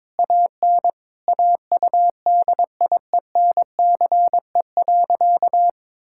Morse code, heard as a keyed tone, about 21 wpm, 700 Hz. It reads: AN AUDIENCE.
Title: Sentences from Top 1000 Words - Review Part 5 - 22wpm